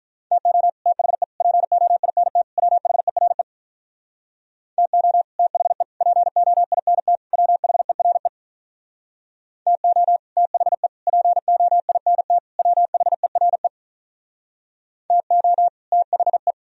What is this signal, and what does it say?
Morse code, heard as a keyed tone, about 46 wpm, 700 Hz. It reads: TO THE POINT WHERE TO THE POINT WHERE TO THE POINT WHERE TO THE